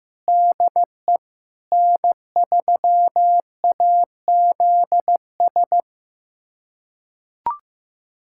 Morse code, heard as a keyed tone, about 15 wpm, 700 Hz. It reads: DE N3AZS
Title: USA Callsign DE Callsign - 5 Characters - 15wpm